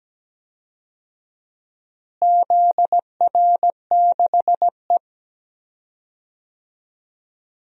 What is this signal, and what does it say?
Morse code, heard as a keyed tone, about 17 wpm, 700 Hz. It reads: ZR6E
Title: DXCC-100 Call Signs - 4 Characters - Repeated 3x - 17wpm